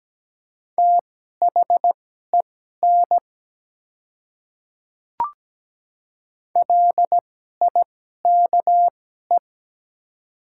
Morse code, heard as a keyed tone, about 17 wpm, spaced as for 12 wpm, 700 Hz. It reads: THEN LIKE
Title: 2x Character Spacing - Top 100 Words - 17wpm